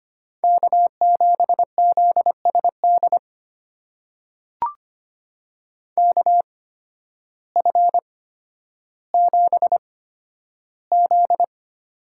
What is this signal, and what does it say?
Morse code, heard as a keyed tone, about 25 wpm, spaced as for 12 wpm, 700 Hz. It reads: K7ZSD KF7Z
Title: USA Call Signs - 5 Characters - 8x Character Spacing - 25wpm